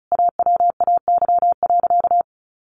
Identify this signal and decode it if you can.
Morse code, heard as a keyed tone, about 34 wpm, 700 Hz. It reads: AWAY.
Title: Sentences from Top 1100 Words 35wpm